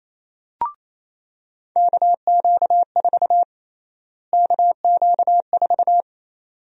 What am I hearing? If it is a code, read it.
Morse code, heard as a keyed tone, about 28 wpm, 700 Hz. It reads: KQ4 KQ4
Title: USA Call Sign Prefixes - Repeated 2x - 28wpm